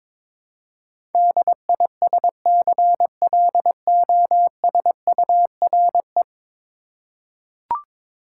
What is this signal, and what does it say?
Morse code, heard as a keyed tone, about 22 wpm, 700 Hz. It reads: DISCLOSURE
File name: practice / 22wpm / 10 Letter Words - First 3 Letters Given - 22wpm